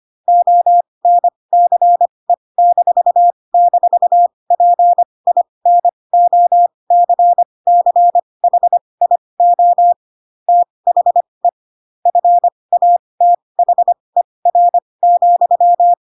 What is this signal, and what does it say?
Morse code, heard as a keyed tone, about 25 wpm, spaced as for 20 wpm, 700 Hz. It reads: ONCE--PINOCCHIO THE FATHER,
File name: Pinocchio / Pinocchio ch3 15 25 ik1hge